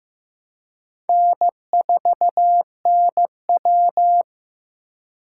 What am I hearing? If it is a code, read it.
Morse code, heard as a keyed tone, about 15 wpm, 700 Hz. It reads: N4NW
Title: K1USN SST QSO Practice - 15wpm